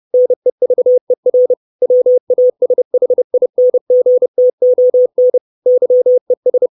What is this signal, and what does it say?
Morse code, heard as a keyed tone, about 30 wpm, 500 Hz. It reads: NEVER WASHINGTON YES
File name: 632 Words 30WPM